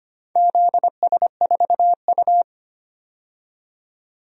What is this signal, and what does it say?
Morse code, heard as a keyed tone, about 25 wpm, 700 Hz. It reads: ZS4U
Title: DXCC-100 Call Signs - 4 Characters - 25wpm